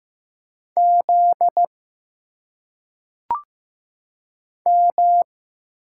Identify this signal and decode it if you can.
Morse code, heard as a keyed tone, about 15 wpm, 700 Hz. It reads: Z M
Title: Single Letters - Phonetic Pronunciation 15wpm